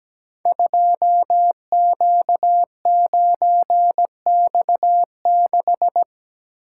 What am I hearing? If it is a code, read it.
Morse code, heard as a keyed tone, about 17 wpm, 700 Hz. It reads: 2Q9X6
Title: Code Groups - 5 Letter:Numbers - 17wpm